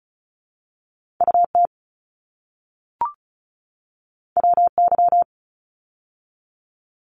Morse code, heard as a keyed tone, about 36 wpm, 700 Hz. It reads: UT WY
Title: ARRL Sections 35wpm